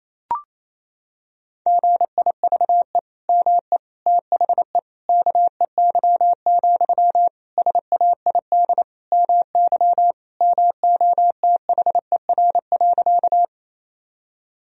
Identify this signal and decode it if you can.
Morse code, heard as a keyed tone, about 28 wpm, 700 Hz. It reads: GIVE ME THE KEY, SAID MY MOTHER.
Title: Sentences from Top 1000 Words - Review Part 4 - 28wpm